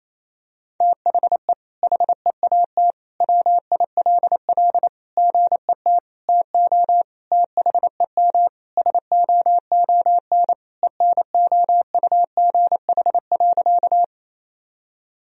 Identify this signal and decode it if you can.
Morse code, heard as a keyed tone, about 28 wpm, 700 Hz. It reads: THE HEAT WILL GET TO THEM SOON ENOUGH.
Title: Sentences from Top 1000 Words - Review Part 3 - 28wpm